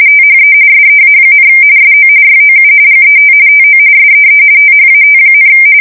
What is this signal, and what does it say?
RTTY, about 45.45 baud. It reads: THE SEA, SO YOU GET TO SEE THAT,
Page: 170 Shift 2125/2295